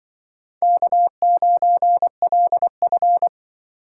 Morse code, heard as a keyed tone, about 25 wpm, 700 Hz. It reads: K9LF
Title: Speed Racing - USA 1x2 Call Signs 20wpm